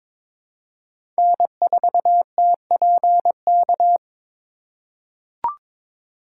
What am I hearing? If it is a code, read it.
Morse code, heard as a keyed tone, about 22 wpm, 700 Hz. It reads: N4TPK
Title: USA Call Signs - 5 Characters - 2x Character Spacing - 22wpm